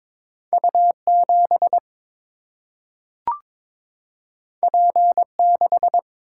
Morse code, heard as a keyed tone, about 22 wpm, 700 Hz. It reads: U 7 P 6
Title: Single-Letter Single-Number 22wpm